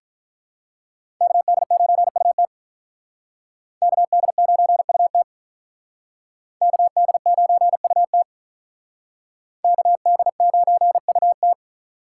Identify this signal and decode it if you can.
Morse code, heard as a keyed tone, about 42 wpm, 700 Hz. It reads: KD9UT KD9UT KD9UT KD9UT
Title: Speed Racing - USA 2x2 Call Signs 35wpm